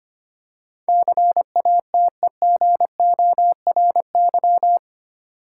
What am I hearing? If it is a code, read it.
Morse code, heard as a keyed tone, about 25 wpm, 700 Hz. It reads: CATEGORY
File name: Eight Letter Words - 25wpm